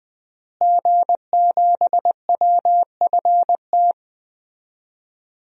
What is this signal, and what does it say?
Morse code, heard as a keyed tone, about 20 wpm, 700 Hz. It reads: G7WFT
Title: DXCC-100 Call Signs - 5 Characters - Repeated 2x - 20wpm